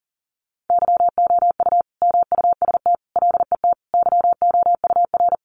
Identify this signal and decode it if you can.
Morse code, heard as a keyed tone, about 40 wpm, 700 Hz. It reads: YOU MUST LET YOUR
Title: Sentences from Top 1500 Words 40wpm